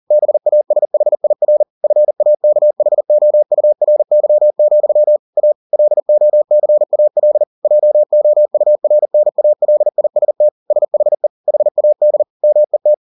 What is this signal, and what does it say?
Morse code, heard as a keyed tone, about 40 wpm, 600 Hz. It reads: BASHIR FAKHOURY, A LOCAL JOURNALIST SHE HAD MET